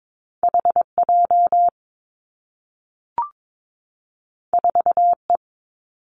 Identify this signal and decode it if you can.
Morse code, heard as a keyed tone, about 22 wpm, 700 Hz. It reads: H J 4 E
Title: Two Challenging Letter-Number Combinations 22wpm